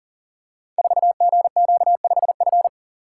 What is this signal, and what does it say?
Morse code, heard as a keyed tone, about 39 wpm, 700 Hz. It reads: 4GQ5F
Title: Code Groups - 5 Letter:Numbers - 40wpm